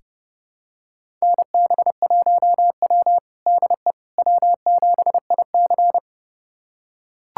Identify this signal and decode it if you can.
Morse code, heard as a keyed tone, about 30 wpm, 700 Hz. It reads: NB1W DE W7IC
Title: USA Callsign DE Callsign - 4 Characters - 30wpm